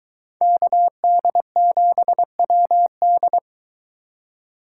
Morse code, heard as a keyed tone, about 22 wpm, 700 Hz. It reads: KD7WD